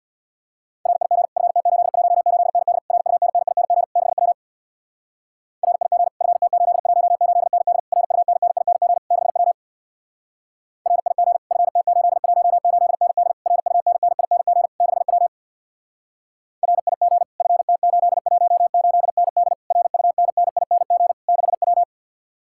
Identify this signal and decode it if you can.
Morse code, heard as a keyed tone, about 66 wpm, 700 Hz. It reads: RIG FT818ND RUNNING 6W RIG FT818ND RUNNING 6W RIG FT818ND RUNNING 6W RIG FT818ND RUNNING 6W